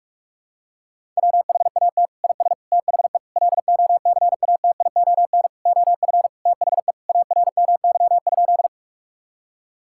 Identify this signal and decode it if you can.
Morse code, heard as a keyed tone, about 46 wpm, 700 Hz. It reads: WHAT IS THE LOCATION OF THE ARMY?
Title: Sentences from Top 1000 Words - Review Part 6 - 45wpm